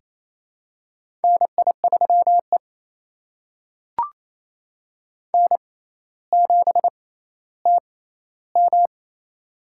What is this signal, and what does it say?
Morse code, heard as a keyed tone, about 28 wpm, spaced as for 14 wpm, 700 Hz. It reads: NI3E N7TM
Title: USA Call Signs - 4 Characters - 6x Character Spacing - 28wpm